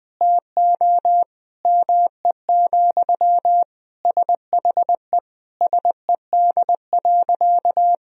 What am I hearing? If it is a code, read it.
Morse code, heard as a keyed tone, about 21 wpm, 700 Hz. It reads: TO ME, SHE SED.
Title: Sentences from Top 500 Words and Abbreviations 20wpm